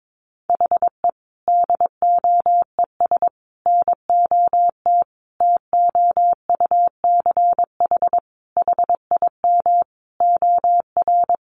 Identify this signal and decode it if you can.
Morse code, heard as a keyed tone, about 22 wpm, 700 Hz. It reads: HE DOES NOT TOUCH HIM OR
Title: Sentences from Top 1400 Words 22wpm